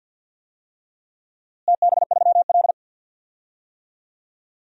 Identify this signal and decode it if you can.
Morse code, heard as a keyed tone, about 48 wpm, 700 Hz. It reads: TB3L
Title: DXCC-100 Call Signs - 4 Characters - Repeated 3x - 50wpm